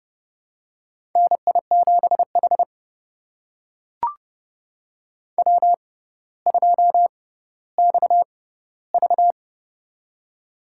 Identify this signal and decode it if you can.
Morse code, heard as a keyed tone, about 30 wpm, spaced as for 15 wpm, 700 Hz. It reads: NI7H W2XV
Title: USA Call Signs - 4 Characters - 6x Character Spacing - 30wpm